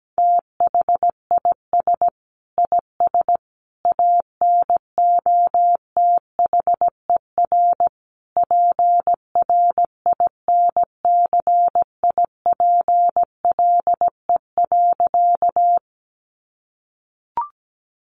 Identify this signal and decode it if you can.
Morse code, heard as a keyed tone, about 17 wpm, 700 Hz. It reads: THIS IS ANOTHER PRINCIPLE.
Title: Sentences from Top 1400 Words 17wpm